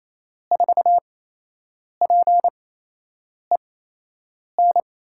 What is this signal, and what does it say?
Morse code, heard as a keyed tone, about 28 wpm, spaced as for 8 wpm, 700 Hz. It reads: 4PEN
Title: USA Call Signs - 6 Characters - 8x Character Spacing - 28wpm